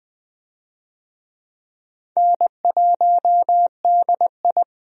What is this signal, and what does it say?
Morse code, heard as a keyed tone, about 20 wpm, 700 Hz. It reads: N1DI
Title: USA Call Signs - 4 Characters - 20wpm